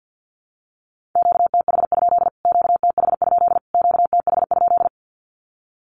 Morse code, heard as a keyed tone, about 48 wpm, 700 Hz. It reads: QTH? QTH? QTH?
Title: Q-Codes Repeated 3x 50wpm